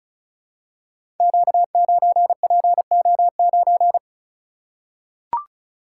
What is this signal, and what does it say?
Morse code, heard as a keyed tone, about 36 wpm, 700 Hz. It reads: Q9PO9
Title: Code Groups - 5 Letter:Numbers - 35wpm